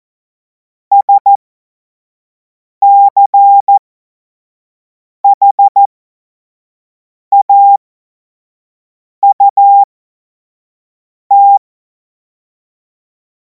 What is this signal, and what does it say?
Morse code, heard as a keyed tone, about 14 wpm, spaced as for 5 wpm, 800 Hz. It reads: SCHAUT
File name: Morsekurs / Witze / Witz 17. Februar 2026 25 F70